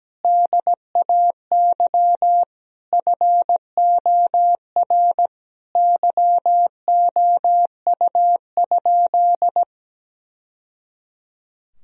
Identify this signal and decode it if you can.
Morse code, heard as a keyed tone, about 17 wpm, 700 Hz. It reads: DAY FOR YOU?
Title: Sentences from Top 1000 Words - Review Part 2 - 17wpm